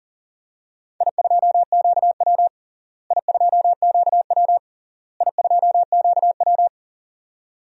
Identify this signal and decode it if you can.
Morse code, heard as a keyed tone, about 37 wpm, 700 Hz. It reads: I2QW I2QW I2QW
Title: DXCC-100 Call Signs - 4 Characters - Repeated 3x - 40wpm